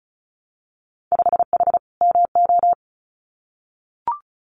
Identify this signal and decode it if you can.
Morse code, heard as a keyed tone, about 36 wpm, 700 Hz. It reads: 5H MO
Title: Winter Field Day Exchanges Encore 35wpm